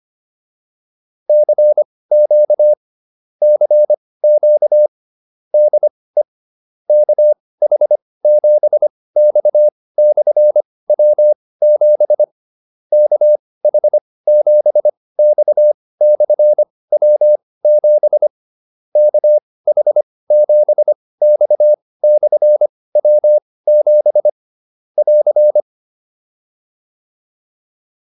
Morse code, heard as a keyed tone, about 25 wpm, spaced as for 18 wpm, 600 Hz. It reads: CQ CQ DE KH7X/W7 KH7X/W7 KH7X/W7 <AR>
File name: QSO104 18